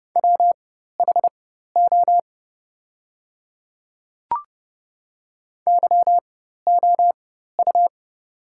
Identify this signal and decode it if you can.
Morse code, heard as a keyed tone, about 30 wpm, spaced as for 14 wpm, 690 Hz. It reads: WHO YOU
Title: 4x Character Spacing - Top 100 Words - 30wpm